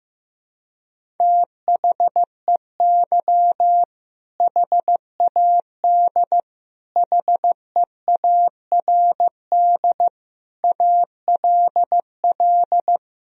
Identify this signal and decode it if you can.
Morse code, heard as a keyed tone, about 16 wpm, 700 Hz. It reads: THEY HAD HEARD ALL
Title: Sentences from Top 400 Words and Abbreviations 15wpm